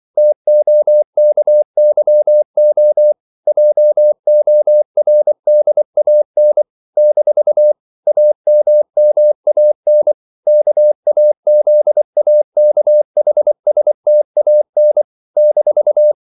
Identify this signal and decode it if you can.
Morse code, heard as a keyed tone, about 24 wpm, 600 Hz. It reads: TOKYO JORDAN - AMMAN KAZAKHSTAN -
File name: CW / Practice / NationalCapitals24